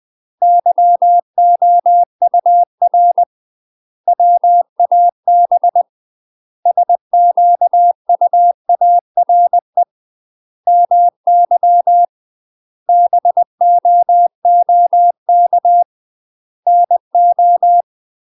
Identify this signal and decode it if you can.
Morse code, heard as a keyed tone, about 20 wpm, 700 Hz. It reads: YOUR WAB SQUARE MY BOOK NO
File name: UK MASTER QSOs TRAINING / 20WPM / UK MASTER QSOs 0029